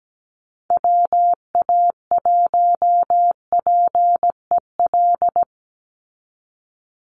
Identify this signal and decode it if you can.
Morse code, heard as a keyed tone, about 17 wpm, 700 Hz. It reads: WA1PEL
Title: USA Call Signs 17wpm